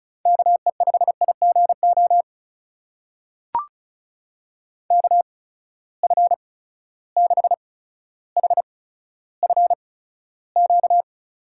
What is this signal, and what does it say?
Morse code, heard as a keyed tone, about 36 wpm, spaced as for 15 wpm, 700 Hz. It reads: KE5IGO KF6HFQ
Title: USA Call Signs - 6 Characters - 8x Character Spacing - 35wpm